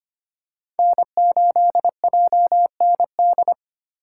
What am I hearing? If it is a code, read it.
Morse code, heard as a keyed tone, about 25 wpm, 700 Hz. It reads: N8JND
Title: USA Call Signs - 4x Character Spacing - 25wpm